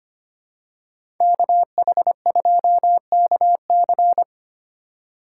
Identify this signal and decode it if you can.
Morse code, heard as a keyed tone, about 25 wpm, 700 Hz. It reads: KH2KC
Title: USA Call Signs - 2x Character Spacing - 25wpm